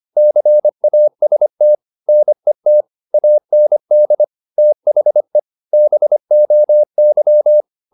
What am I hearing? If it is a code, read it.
Morse code, heard as a keyed tone, about 25 wpm, 600 Hz. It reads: CAST NET AND THE BOY
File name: the old man and the sea part 1 25wpm